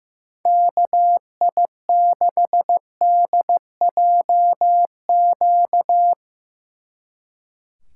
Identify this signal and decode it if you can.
Morse code, heard as a keyed tone, about 16 wpm, 700 Hz. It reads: KI6DJQ
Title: Speed Racing - USA 2x3 Call Signs 15wpm